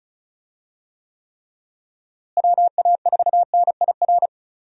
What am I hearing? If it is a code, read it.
Morse code, heard as a keyed tone, about 34 wpm, 700 Hz. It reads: WA4NIR
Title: USA Call Signs - 6 Characters - Repeated 3x - 35wpm